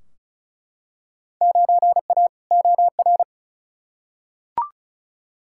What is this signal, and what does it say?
Morse code, heard as a keyed tone, about 36 wpm, 700 Hz. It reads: 9A OR
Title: Field Day Exchanges Encore - 35wpm